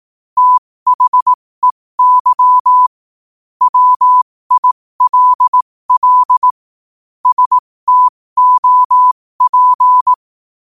Morse code, heard as a keyed tone, about 18 wpm, spaced as for 15 wpm, 1000 Hz. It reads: THEY WILL STOP
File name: qotdfiles / 15wpm